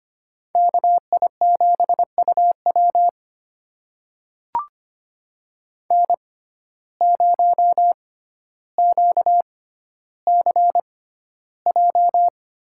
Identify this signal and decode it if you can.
Morse code, heard as a keyed tone, about 25 wpm, spaced as for 13 wpm, 700 Hz. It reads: KI7UW N0QCJ
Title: USA Call Signs - 5 Characters - 6x Character Spacing - 25wpm